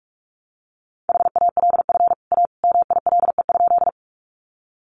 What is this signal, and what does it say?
Morse code, heard as a keyed tone, about 44 wpm, 700 Hz. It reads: HALF A MILE?
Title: Sentences from Top 400 Words 45wpm